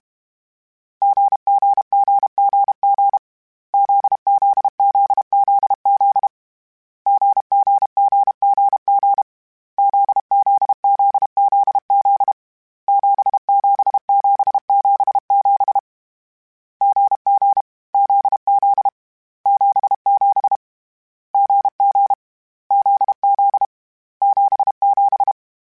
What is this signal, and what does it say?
Morse code, heard as a keyed tone, about 32 wpm, 800 Hz. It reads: GGGGG ZZZZZ GGGGG ZZZZZ 77777 GG ZZ 77 GG ZZ 77
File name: Difficult Combos / Difficult Combos-mp3s / GZ GZ7 @30